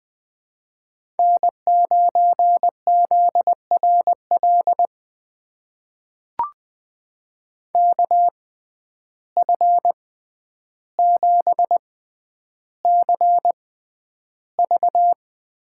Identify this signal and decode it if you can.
Morse code, heard as a keyed tone, about 20 wpm, spaced as for 10 wpm, 700 Hz. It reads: N9ZRL KF7CV
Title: USA Call Signs - 5 Characters - 6x Character Spacing - 20wpm